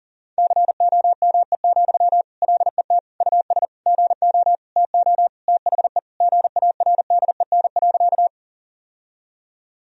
Morse code, heard as a keyed tone, about 40 wpm, 700 Hz. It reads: COME, LET US GO TO THE GARDEN.
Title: Sentences from Top 1000 Words - Review Part 4 - 40wpm